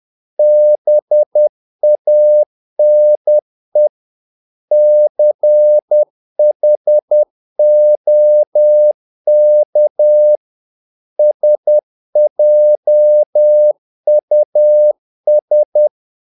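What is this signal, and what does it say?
Morse code, heard as a keyed tone, about 10 wpm, 600 Hz. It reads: BANE CHOK SJUS
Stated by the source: Morse practice of short Danish words at 10 wpm, from CW med Gnister.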